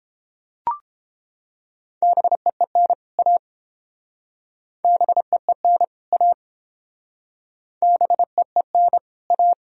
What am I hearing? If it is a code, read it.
Morse code, heard as a keyed tone, about 30 wpm, 700 Hz. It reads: BEEN A BEEN A BEEN A